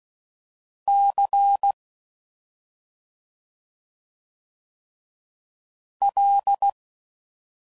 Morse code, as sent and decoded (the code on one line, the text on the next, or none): -.-. / .-..
C L